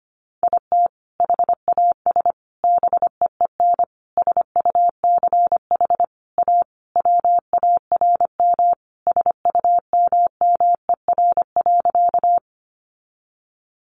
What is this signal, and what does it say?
Morse code, heard as a keyed tone, about 25 wpm, 700 Hz. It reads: IT HAS BEEN SUCH A WARM SUMMER.
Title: Sentences from Top 1100 Words 25wpm